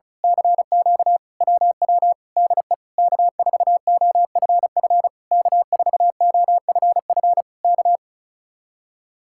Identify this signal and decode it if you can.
Morse code, heard as a keyed tone, about 32 wpm, 700 Hz. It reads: CQ WW DE K4OFF K4OFF K
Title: Speed-Racing Common QSO Phrases Encore - 35wpm